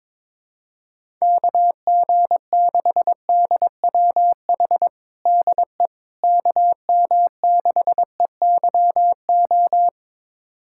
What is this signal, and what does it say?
Morse code, heard as a keyed tone, about 22 wpm, 700 Hz. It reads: KG6DWH DE KM6EYO
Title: USA Callsign DE Callsign - 6 Characters - 22wpm